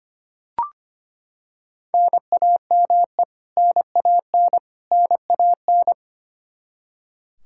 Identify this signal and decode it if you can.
Morse code, heard as a keyed tone, about 25 wpm, 700 Hz. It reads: NAME NAN NAN
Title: Practice QSO - Exchange 1 - Beginning - 25wpm